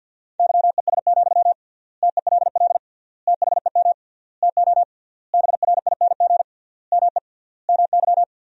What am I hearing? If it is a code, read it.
Morse code, heard as a keyed tone, about 52 wpm, 700 Hz. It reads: YES, TELL THEM TO BRING ME MY
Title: 3x Word-Spacing - Sentences from Top 500 Words - 50wpm